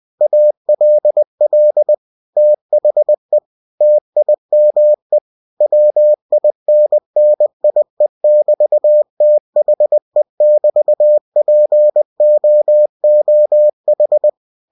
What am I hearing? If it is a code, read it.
Morse code, heard as a keyed tone, about 20 wpm, 600 Hz. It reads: ALL THE TIME WINNIE<BT>THE<BT>POOH
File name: cw-books / winnie-the-pooh / winnie-20wpm-page-0042